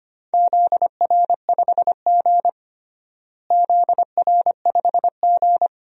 Morse code, heard as a keyed tone, about 25 wpm, 700 Hz. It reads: ZR5G ZR5G
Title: DXCC-100 Call Signs - 4 Characters - Repeated 2x - 25wpm